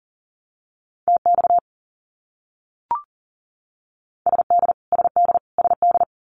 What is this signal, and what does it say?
Morse code, heard as a keyed tone, about 40 wpm, 700 Hz. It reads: TX SD SD SD
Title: US States Repeated 3x 40wpm